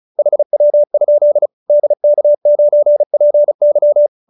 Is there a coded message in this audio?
Yes: Morse code from 0.2 to 4.1 s, heard as a keyed tone, about 34 wpm, 590 Hz. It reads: HW? DK9PY